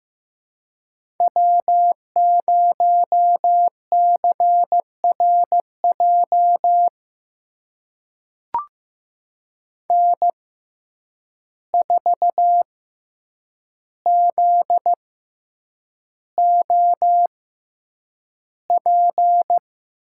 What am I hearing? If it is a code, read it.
Morse code, heard as a keyed tone, about 15 wpm, spaced as for 8 wpm, 700 Hz. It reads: W0CRJ N4ZOP
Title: USA Call Signs - 5 Characters - 6x Character Spacing - 15wpm